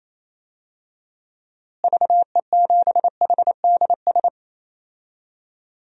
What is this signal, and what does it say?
Morse code, heard as a keyed tone, about 28 wpm, 700 Hz. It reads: VE7HDS
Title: DXCC-100 Call Signs - 6 Characters - Repeated 2x - 28wpm